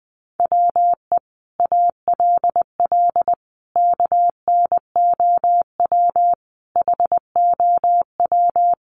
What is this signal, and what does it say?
Morse code, heard as a keyed tone, about 20 wpm, 700 Hz. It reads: WE ALL KNOW HOW
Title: Sentences from Top 1200 Words 20wpm